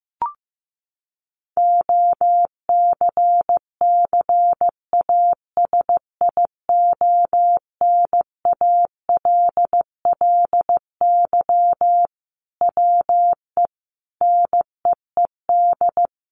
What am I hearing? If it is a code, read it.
Morse code, heard as a keyed tone, about 15 wpm, 700 Hz. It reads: OCCASIONALLY WE NEED
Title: Sentences from Top 1300 Words 15wpm